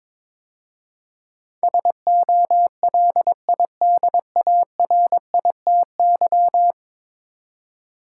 Morse code, heard as a keyed tone, about 22 wpm, 700 Hz. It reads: SOLIDARITY
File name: practice / 22wpm / 10 Letter Words - First 3 Letters Given - 22wpm